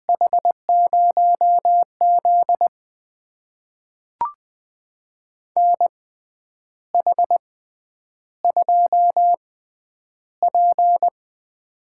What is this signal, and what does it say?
Morse code, heard as a keyed tone, about 20 wpm, spaced as for 9 wpm, 700 Hz. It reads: H0Z NH2P
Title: USA Call Signs - 4 Characters - 6x Character Spacing - 20wpm